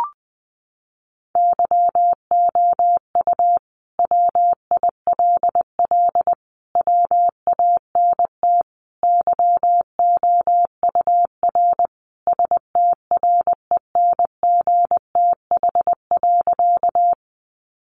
Morse code, heard as a keyed tone, about 20 wpm, 700 Hz. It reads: YOU WILL WANT YOUR STRENGTH.
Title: Sentences from Top 1300 Words 20wpm